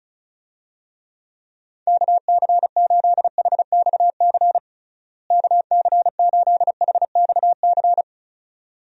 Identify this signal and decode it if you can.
Morse code, heard as a keyed tone, about 34 wpm, 700 Hz. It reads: KC8HXC KC8HXC
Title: USA Call Signs - 6 Characters - Repeated 2x - 35wpm